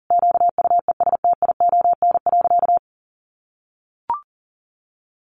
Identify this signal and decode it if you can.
Morse code, heard as a keyed tone, about 40 wpm, 700 Hz. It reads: QUESTION.
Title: Sentences from Top 600 Words 40wpm